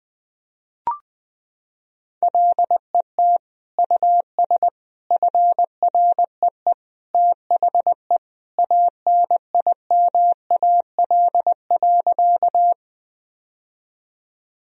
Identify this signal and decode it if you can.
Morse code, heard as a keyed tone, about 21 wpm, 700 Hz. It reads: LET US FREE THE ANIMAL.
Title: Sentences from Top 1000 Words - Review Part 1 - 20wpm